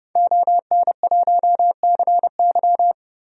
Morse code, heard as a keyed tone, about 30 wpm, 700 Hz. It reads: ON1CY
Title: DXCC-100 Call Signs - 5 Characters - 30wpm